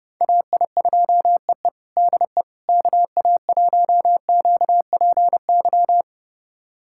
Morse code, heard as a keyed tone, about 32 wpm, 700 Hz. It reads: AI2EE DE KA1QPY
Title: USA Callsign DE Callsign 30wpm